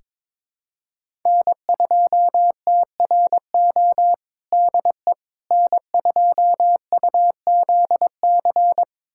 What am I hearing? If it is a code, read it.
Morse code, heard as a keyed tone, about 22 wpm, 700 Hz. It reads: N2TRO DE N2UZC
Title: USA Callsign DE Callsign - 5 Characters - 22wpm